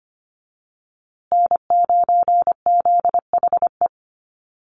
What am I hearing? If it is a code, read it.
Morse code, heard as a keyed tone, about 25 wpm, 700 Hz. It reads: N9ZHE
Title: USA Call Signs Repeated 2x 25wpm